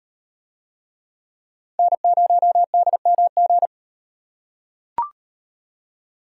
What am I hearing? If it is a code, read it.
Morse code, heard as a keyed tone, about 38 wpm, 700 Hz. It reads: N0DMG